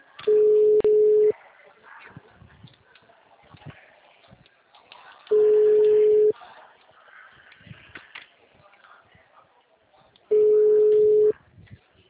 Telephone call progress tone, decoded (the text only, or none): ringback tone